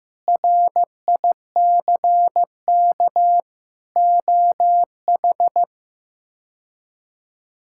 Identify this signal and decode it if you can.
Morse code, heard as a keyed tone, about 16 wpm, 700 Hz. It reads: RICK OH
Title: K1USN SST QSO Practice - 15wpm